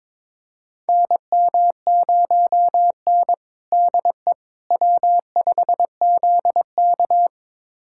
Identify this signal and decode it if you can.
Morse code, heard as a keyed tone, about 22 wpm, 700 Hz. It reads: NM0N DE W5ZK
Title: USA Callsign DE Callsign - 4 Characters - 22wpm